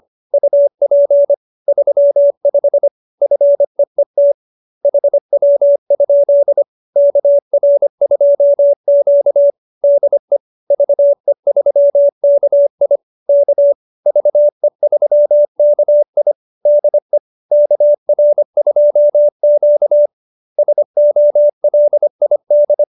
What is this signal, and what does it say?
Morse code, heard as a keyed tone, about 25 wpm, 570 Hz. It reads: UP 35 FEET HW? KR2Q DE VE3KI K VE3KI DE KR2Q SOLID